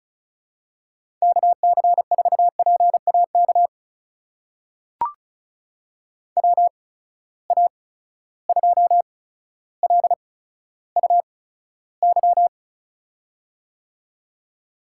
Morse code, heard as a keyed tone, about 34 wpm, spaced as for 15 wpm, 700 Hz. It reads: KC4PAK WA2LUY
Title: USA Call Signs - 6 Characters - 8x Character Spacing - 35wpm